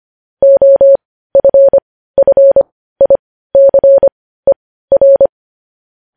Morse code, heard as a keyed tone, about 25 wpm, spaced as for 15 wpm, 550 Hz. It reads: OFFICER
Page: news headlines in morse code at 15 wpm